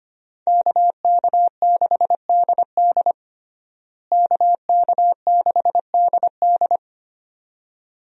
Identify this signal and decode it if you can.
Morse code, heard as a keyed tone, about 25 wpm, 700 Hz. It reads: KK6DD KK6DD
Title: USA Call Signs - 5 Characters - Repeated 2x - 25wpm